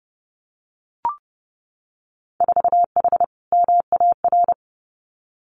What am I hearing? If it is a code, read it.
Morse code, heard as a keyed tone, about 30 wpm, 700 Hz. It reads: 4H MAR
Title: Winter Field Day Exchanges Encore 30wpm